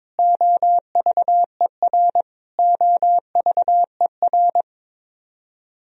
Morse code, heard as a keyed tone, about 21 wpm, 700 Hz. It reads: OVER OVER
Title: Top 100 Words Repeated 22wpm